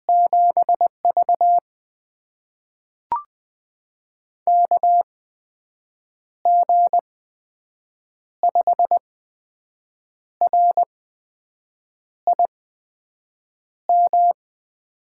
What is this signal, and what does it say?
Morse code, heard as a keyed tone, about 20 wpm, spaced as for 6 wpm, 700 Hz. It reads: 7V KG5RIM
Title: USA Call Signs - 8x Character Spacing - 20wpm